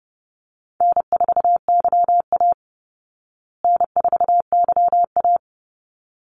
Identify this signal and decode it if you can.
Morse code, heard as a keyed tone, about 30 wpm, 700 Hz. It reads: N4YA N4YA
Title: Call Signs Repeated 3x - 30wpm